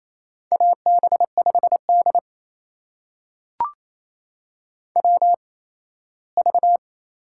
Morse code, heard as a keyed tone, about 28 wpm, spaced as for 17 wpm, 700 Hz. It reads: AB5D WV
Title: USA Call Signs - 4 Characters - 8x Character Spacing - 28wpm